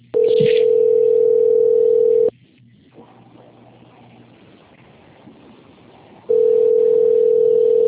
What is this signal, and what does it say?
Ringback tone (North America), 440 + 480 Hz.